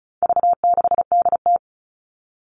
Morse code, heard as a keyed tone, about 34 wpm, 700 Hz. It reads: V6DT
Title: All Cut Numbers - 4-Digit 35wpm